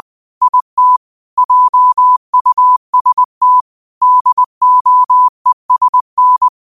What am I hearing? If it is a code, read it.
Morse code, heard as a keyed tone, about 20 wpm, 1000 Hz. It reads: IT JUST DOESN
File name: 20wpm